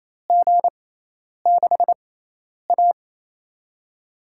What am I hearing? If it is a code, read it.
Morse code, heard as a keyed tone, about 28 wpm, spaced as for 10 wpm, 700 Hz. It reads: G6A
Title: USA Call Signs - 6x Character Spacing - 28wpm